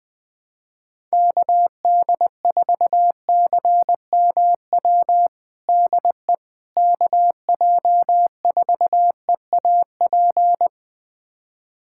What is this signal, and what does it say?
Morse code, heard as a keyed tone, about 20 wpm, 700 Hz. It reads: KD4CMW DE KJ4EAP
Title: USA Callsign DE Callsign 20wpm